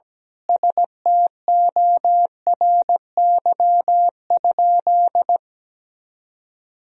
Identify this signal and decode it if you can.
Morse code, heard as a keyed tone, about 17 wpm, 700 Hz. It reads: STORY?
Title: Sentences from Top 400 Words and Abbreviations 17wpm